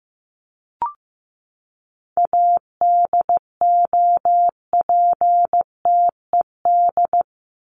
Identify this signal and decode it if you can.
Morse code, heard as a keyed tone, about 15 wpm, 700 Hz. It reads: ADOPTED
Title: ED Suffix - Top 100 Words with Base Words 15wpm